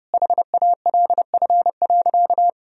Morse code, heard as a keyed tone, about 30 wpm, 700 Hz. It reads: HALF.
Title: Sentences from Top 1000 Words - Review Part 2 - 30wpm